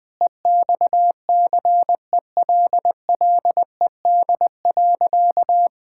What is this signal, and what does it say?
Morse code, heard as a keyed tone, about 20 wpm, 700 Hz. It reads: EXCELLED.
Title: Sentences with 6-Word Phrases - Spoken Beginning - 20wpm